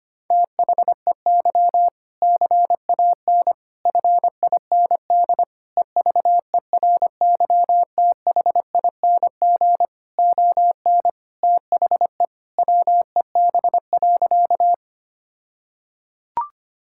Morse code, heard as a keyed tone, about 25 wpm, 700 Hz. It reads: THEY CAN FIND EVERYTHING ON THE WEB.
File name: practice / 25wpm / All-Age Appropriate Jokes - 25wpm